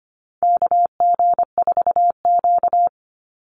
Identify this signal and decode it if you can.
Morse code, heard as a keyed tone, about 25 wpm, 700 Hz. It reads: KG4Q
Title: Call Signs - 25wpm